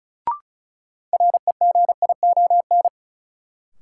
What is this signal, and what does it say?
Morse code, heard as a keyed tone, about 36 wpm, 700 Hz. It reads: REGION
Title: Sentences from Top 1500 Words - New Word Drill Alt - 35wpm